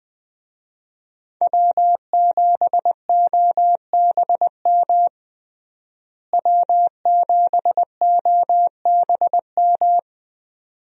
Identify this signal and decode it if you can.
Morse code, heard as a keyed tone, about 20 wpm, 700 Hz. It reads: W7OBM W7OBM
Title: USA Call Signs - 5 Characters - Repeated 2x - 20wpm